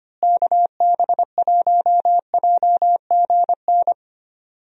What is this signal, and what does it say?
Morse code, heard as a keyed tone, about 25 wpm, 700 Hz. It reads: KB1JGN
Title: USA Call Signs - 6 Characters - Repeated 3x - 25wpm